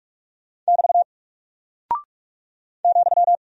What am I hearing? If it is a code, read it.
Morse code, heard as a keyed tone, about 44 wpm, 700 Hz. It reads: <BT> ,
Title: Single Character (Letter:Number:Punctuation:Prosign) 45wpm